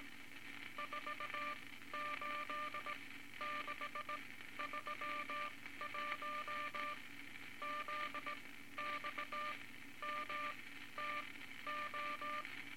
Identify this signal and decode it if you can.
Morse code, heard as a keyed tone, about 18 wpm, spaced as for 13 wpm, 1190 Hz, with noise. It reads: 48631 ZXMTO
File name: Morse Code Records / 18 A LCM 164 Ten Words Per Minute